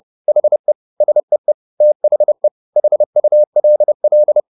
Morse code, heard as a keyed tone, about 30 wpm, 600 Hz. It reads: HE SEE THE HULL